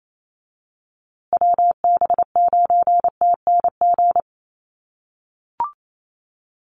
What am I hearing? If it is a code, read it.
Morse code, heard as a keyed tone, about 28 wpm, 700 Hz. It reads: WB9TNG
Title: USA Call Signs Repeated 3x 28wpm